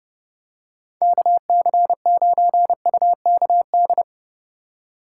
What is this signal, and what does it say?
Morse code, heard as a keyed tone, about 30 wpm, 700 Hz. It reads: KC9UKD
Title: USA Call Signs - 4x Character Spacing - 30wpm